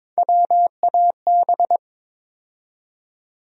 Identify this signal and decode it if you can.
Morse code, heard as a keyed tone, about 21 wpm, 700 Hz. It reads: WAB
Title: USA Call Sign Postfixes - 3 Letters 22wpm